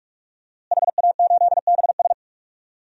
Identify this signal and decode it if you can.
Morse code, heard as a keyed tone, about 42 wpm, 700 Hz. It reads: HA8BS
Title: DXCC-100 Call Signs - 5 Characters - Repeated 2x - 45wpm